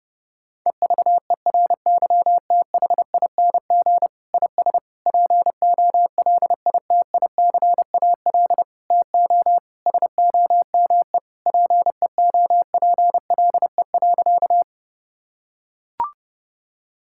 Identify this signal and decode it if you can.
Morse code, heard as a keyed tone, about 30 wpm, 700 Hz. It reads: EVERYTHING IS POLITICAL TO SOME PEOPLE.
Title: Sentences from Top 1000 Words - Review Part 5 - 30wpm